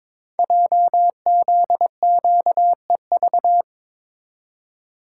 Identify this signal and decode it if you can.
Morse code, heard as a keyed tone, about 22 wpm, 700 Hz. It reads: JZQEV
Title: Code Groups - 5 Letters - 22wpm